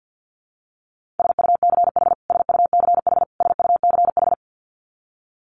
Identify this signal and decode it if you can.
Morse code, heard as a keyed tone, about 50 wpm, 700 Hz. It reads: SUCH SUCH SUCH
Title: Top 200 Words Repeated 50wpm